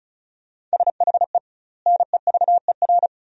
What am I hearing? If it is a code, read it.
Morse code, heard as a keyed tone, about 36 wpm, 700 Hz. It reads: SHE NEVER
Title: 2x Word-Spacing - Sentences from Top 300 Words - 35wpm